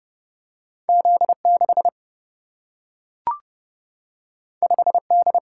Morse code, heard as a keyed tone, about 30 wpm, 700 Hz. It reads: Z 6 5 D
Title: Single-Letter Single-Number 30wpm